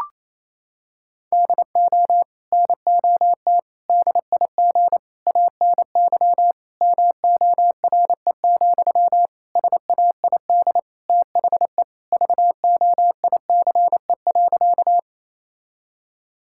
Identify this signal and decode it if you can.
Morse code, heard as a keyed tone, about 28 wpm, 700 Hz. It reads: DO NOT DIG ANY MORE, SAID THE VOICE.
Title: Sentences from Top 1600 Words 28wpm